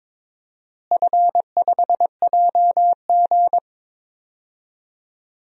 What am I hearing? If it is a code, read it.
Morse code, heard as a keyed tone, about 22 wpm, 700 Hz. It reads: F5JG
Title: DXCC-100 Call Signs - 4 Characters - Repeated 2x - 22wpm